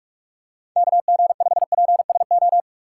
Morse code, heard as a keyed tone, about 44 wpm, 700 Hz. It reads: KG5PSO
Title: USA Call Signs - 6 Characters - 6x Character Spacing - 45wpm